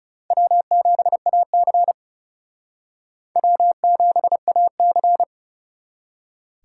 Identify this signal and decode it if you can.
Morse code, heard as a keyed tone, about 40 wpm, 700 Hz. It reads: W7AC W7AC